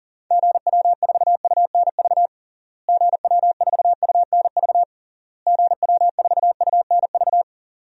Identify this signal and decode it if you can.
Morse code, heard as a keyed tone, about 40 wpm, 700 Hz. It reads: GW4UNV GW4UNV GW4UNV
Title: DXCC-100 Call Signs - 6 Characters - Repeated 3x - 40wpm